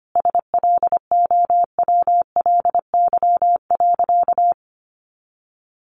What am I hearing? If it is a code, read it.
Morse code, heard as a keyed tone, about 25 wpm, 700 Hz. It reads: SLOWLY.
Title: Easy Phrases 25wpm